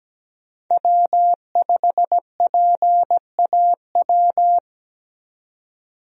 Morse code, heard as a keyed tone, about 17 wpm, 700 Hz. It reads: W5PAW
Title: USA Call Signs - 5 Characters - 17wpm